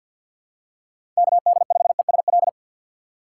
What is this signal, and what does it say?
Morse code, heard as a keyed tone, about 48 wpm, 690 Hz. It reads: KD5ESL
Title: USA Call Signs - 6 Characters - 8x Character Spacing - 50wpm